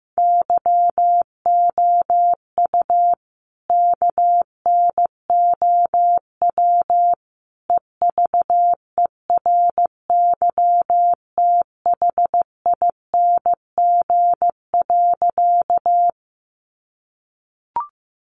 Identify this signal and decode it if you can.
Morse code, heard as a keyed tone, about 15 wpm, 700 Hz. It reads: YOU KNOW EVERYTHING.
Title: Sentences from Top 700 Words 15wpm